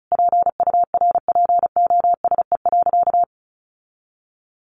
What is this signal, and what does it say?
Morse code, heard as a keyed tone, about 34 wpm, 700 Hz. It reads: PURPOSE.
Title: Sentences from Top 1500 Words 35wpm